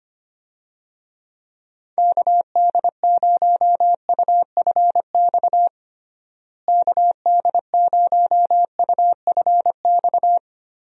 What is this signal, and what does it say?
Morse code, heard as a keyed tone, about 25 wpm, 700 Hz. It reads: KD0UFX KD0UFX
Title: USA Call Signs - 6 Characters - Repeated 2x - 25wpm